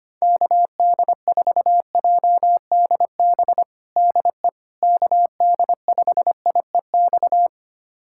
Morse code, heard as a keyed tone, about 25 wpm, 700 Hz. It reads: KD4JDB DE KD5IEX
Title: USA Callsign DE Callsign 25wpm